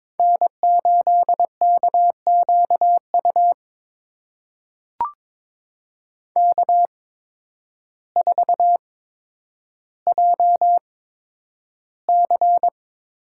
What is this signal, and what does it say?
Morse code, heard as a keyed tone, about 22 wpm, spaced as for 10 wpm, 700 Hz. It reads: N8KQU K4JC
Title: USA Call Signs - 5 Characters - 8x Character Spacing - 22wpm